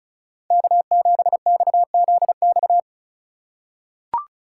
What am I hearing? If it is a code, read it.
Morse code, heard as a keyed tone, about 36 wpm, 700 Hz. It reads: K7XZX
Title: USA Call Signs - 4x Character Spacing - 35wpm